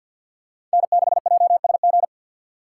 Morse code, heard as a keyed tone, about 52 wpm, 700 Hz. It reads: N6JSG
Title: USA Call Signs - 5 Characters - 2x Character Spacing - 50wpm